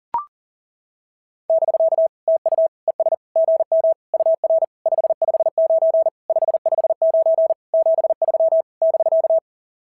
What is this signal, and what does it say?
Morse code, heard as a keyed tone, about 40 wpm, 650 Hz. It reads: <BK> TU ES GM UR 559 559 73 <BK>
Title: SOTA QSO Practice 40wpm